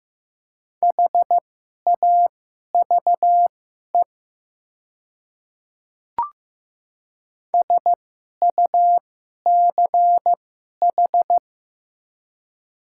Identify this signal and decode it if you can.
Morse code, heard as a keyed tone, about 15 wpm, spaced as for 11 wpm, 700 Hz. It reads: HAVE SUCH
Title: 2x Character Spacing - Top 200 Words - 15wpm